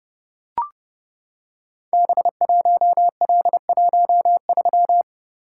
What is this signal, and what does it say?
Morse code, heard as a keyed tone, about 30 wpm, 700 Hz. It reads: B1L13
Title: Code Groups - 5 Letter:Numbers - 30wpm